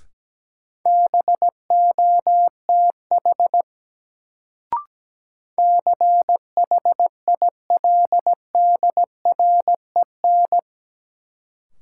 Morse code, heard as a keyed tone, about 17 wpm, 700 Hz. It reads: BOTH CHILDREN
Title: Sentences from Top 300 Words - New Word Drill - 17wpm